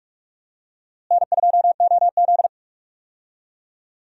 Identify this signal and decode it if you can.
Morse code, heard as a keyed tone, about 46 wpm, 700 Hz. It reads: N2OZ
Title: USA Call Signs - 4 Characters - Repeated 2x - 45wpm